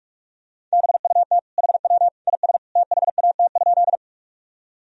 Morse code, heard as a keyed tone, about 46 wpm, 700 Hz. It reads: BUT HW IS THAT?
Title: Sentences from Top 100 Words and Abbreviations 45wpm